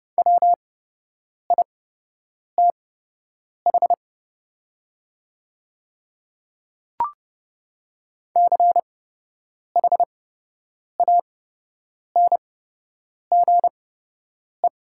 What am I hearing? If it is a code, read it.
Morse code, heard as a keyed tone, about 29 wpm, spaced as for 8 wpm, 700 Hz. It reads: WITH CHANGE
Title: 8x Character Spacing - Top 200 Words - 30wpm